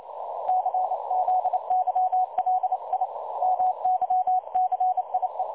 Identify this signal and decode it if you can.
Morse code, heard as a keyed tone, about 30 wpm, 740 Hz. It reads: CZYLI MYCIE
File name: dlazaawansowanych / nagraniazpasma / websdr recording 3565.0kHz